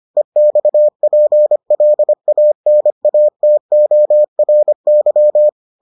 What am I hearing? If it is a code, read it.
Morse code, heard as a keyed tone, about 25 wpm, 600 Hz. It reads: EXPLANATORY